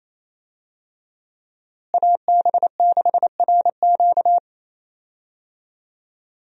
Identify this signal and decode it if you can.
Morse code, heard as a keyed tone, about 28 wpm, 700 Hz. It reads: AB6RQ
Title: USA Call Signs - 5 Characters - Repeated 3x - 28wpm